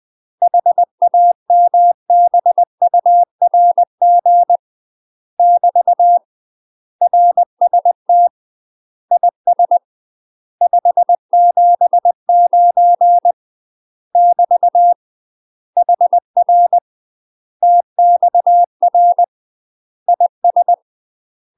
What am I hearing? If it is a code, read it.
Morse code, heard as a keyed tone, about 20 wpm, 700 Hz. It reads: HAMBURG <BT> RST IS 579 <BT> HR TXR IS
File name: UK MASTER QSOs TRAINING / 20WPM / UK MASTER QSOs 0066